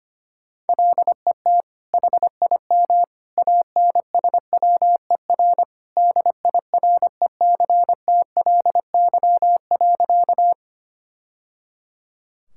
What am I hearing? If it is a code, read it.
Morse code, heard as a keyed tone, about 25 wpm, 700 Hz. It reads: LET HIM ANSWER DIRECTLY.
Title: Sentences from Top 1000 Words - Review Part 5 - 25wpm